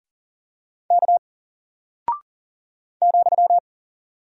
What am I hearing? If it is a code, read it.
Morse code, heard as a keyed tone, about 40 wpm, 700 Hz. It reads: K ,
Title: Single Character (Letter:Number:Punctuation:Prosign) Alt 40wpm